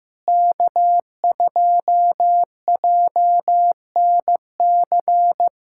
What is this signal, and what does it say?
Morse code, heard as a keyed tone, about 15 wpm, 700 Hz. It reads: K2JNC
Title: USA Callsign DE Callsign - 5 Characters - 15wpm